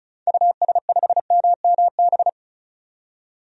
Morse code, heard as a keyed tone, about 34 wpm, 700 Hz. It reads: US5MMB
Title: DXCC-100 Call Signs - 6 Characters - 35wpm